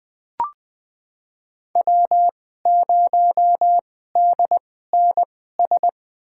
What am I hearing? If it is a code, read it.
Morse code, heard as a keyed tone, about 20 wpm, spaced as for 14 wpm, 700 Hz. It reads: W0DNS
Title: USA Call Signs - 5 Characters - 2x Character Spacing - 20wpm